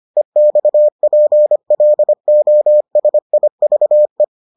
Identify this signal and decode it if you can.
Morse code, heard as a keyed tone, about 25 wpm, 600 Hz. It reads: EXPLOSIVE